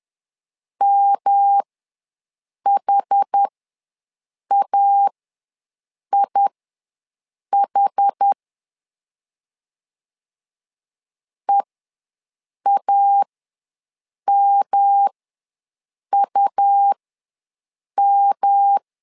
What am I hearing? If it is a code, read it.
Morse code, heard as a keyed tone, about 10 wpm, spaced as for 6 wpm, 800 Hz. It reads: MHAIH EAMUM
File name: LEZIONI / 3 MWRO / 3 M ws 10wpm 2121